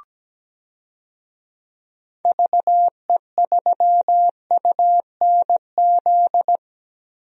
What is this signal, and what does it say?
Morse code, heard as a keyed tone, about 17 wpm, 700 Hz. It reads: VE3UNZ
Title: DXCC-100 Call Signs - 6 Characters - Repeated 2x - 17wpm